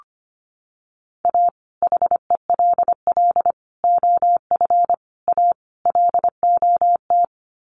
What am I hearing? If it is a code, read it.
Morse code, heard as a keyed tone, about 25 wpm, 700 Hz. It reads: A HELL OF A LOT
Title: Sets of 5 Words Encore 25wpm